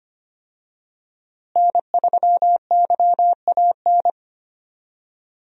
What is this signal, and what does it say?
Morse code, heard as a keyed tone, about 25 wpm, 700 Hz. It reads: N3YAN
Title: USA Call Signs - 5 Characters - Repeated 3x - 25wpm